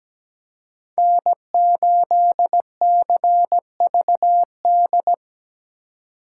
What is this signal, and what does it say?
Morse code, heard as a keyed tone, about 20 wpm, 700 Hz. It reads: N8CVD
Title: Speed Racing - USA 1x3 Call Signs 17wpm